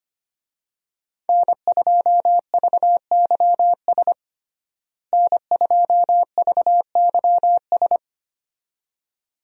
Morse code, heard as a keyed tone, about 25 wpm, 700 Hz. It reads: N2VYS N2VYS
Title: USA Call Signs - 5 Characters - Repeated 2x - 25wpm